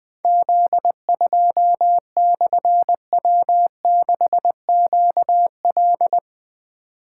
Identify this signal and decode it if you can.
Morse code, heard as a keyed tone, about 20 wpm, 700 Hz. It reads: Z2/W6QL
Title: Difficult Call Signs 20wpm